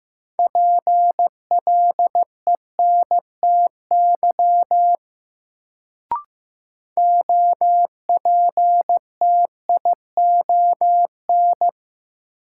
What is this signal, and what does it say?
Morse code, heard as a keyed tone, about 15 wpm, 700 Hz. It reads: PLENTY OPTION
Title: Sentences from Top 1300 Words - New Word Drill Alt - 15wpm